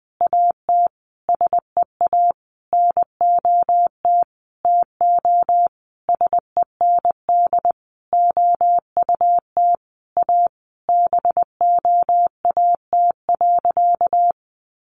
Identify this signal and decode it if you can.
Morse code, heard as a keyed tone, about 20 wpm, 700 Hz. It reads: AT SEA NOT TO SEND OUT A BOAT.
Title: Sentences from Top 1500 Words 20wpm